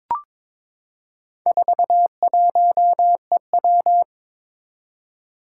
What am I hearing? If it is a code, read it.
Morse code, heard as a keyed tone, about 22 wpm, 700 Hz. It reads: 41EW
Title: Four Challenging Letter-Number Combinations 22wpm